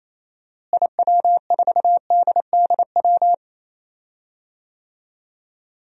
Morse code, heard as a keyed tone, about 28 wpm, 700 Hz. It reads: IW4DDW
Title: DXCC-100 Call Signs - 6 Characters - Repeated 2x - 28wpm